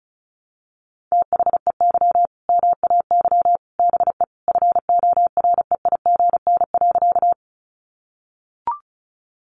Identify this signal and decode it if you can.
Morse code, heard as a keyed tone, about 34 wpm, 700 Hz. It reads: THEY MAY BE FOREIGN.
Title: Sentences from Top 1400 Words 35wpm